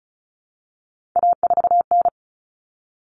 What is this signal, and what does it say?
Morse code, heard as a keyed tone, about 34 wpm, 700 Hz. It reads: A4N
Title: Common Cut Numbers - 3-Digit 35wpm